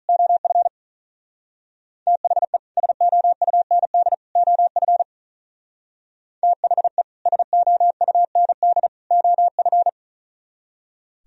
Morse code, heard as a keyed tone, about 41 wpm, 700 Hz. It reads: OF THE SOUND OF THE SOUND OF
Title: Speed-Racing - Sets of 3 Words 35wpm